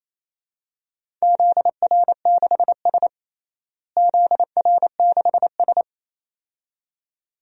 Morse code, heard as a keyed tone, about 28 wpm, 700 Hz. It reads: ZR6S ZR6S
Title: DXCC-100 Call Signs - 4 Characters - Repeated 2x - 28wpm